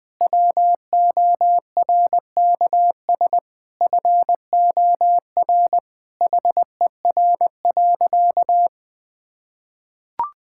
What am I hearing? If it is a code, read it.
Morse code, heard as a keyed tone, about 21 wpm, 700 Hz. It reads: WORKS FOR HER.
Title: Sentences with 4-Word Phrases - 20wpm